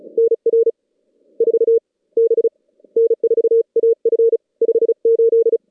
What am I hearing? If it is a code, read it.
Morse code, heard as a keyed tone, about 32 wpm, 440 Hz. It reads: NR 4 B N4AF 58
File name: contests / CW-SS-k5tr- -2100